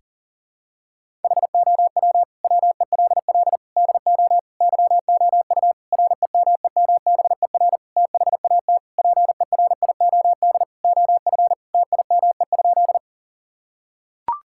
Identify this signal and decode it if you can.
Morse code, heard as a keyed tone, about 40 wpm, 700 Hz. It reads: HOW WELL DO YOU REMEMBER THAT PERIOD OF TIME?
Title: Sentences from Top 1000 Words - Review Part 5 - 40wpm